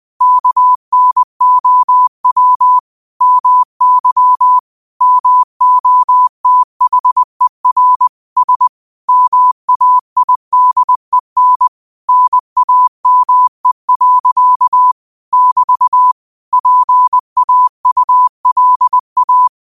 Morse code, heard as a keyed tone, about 20 wpm, 1000 Hz. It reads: KNOW MY MOTHER S MAIDEN NAME. <BT> PAULA